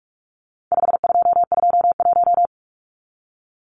Morse code, heard as a keyed tone, about 44 wpm, 700 Hz. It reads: 5221
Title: Numbers - 4-Digit 45wpm